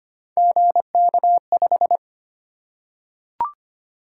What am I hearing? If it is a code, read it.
Morse code, heard as a keyed tone, about 25 wpm, 700 Hz. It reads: GK5
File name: Three Letter-Number Combinations - 25wpm